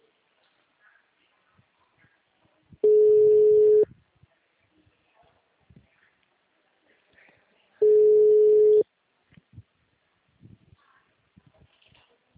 Ringback tone (Europe, 425 Hz).
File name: out- -2012- - - .656